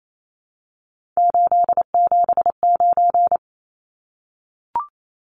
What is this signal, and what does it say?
Morse code, heard as a keyed tone, about 28 wpm, 700 Hz. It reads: 879
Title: Numbers - 3-Digit 28wpm